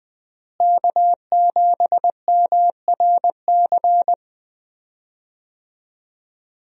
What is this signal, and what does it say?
Morse code, heard as a keyed tone, about 20 wpm, 700 Hz. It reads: K7MRC
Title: USA Call Signs - 5 Characters - Repeated 2x - 20wpm